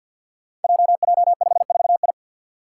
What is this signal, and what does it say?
Morse code, heard as a keyed tone, about 50 wpm, 700 Hz. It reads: JJ54I
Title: Five Challenging Letter-Number Combinations 50wpm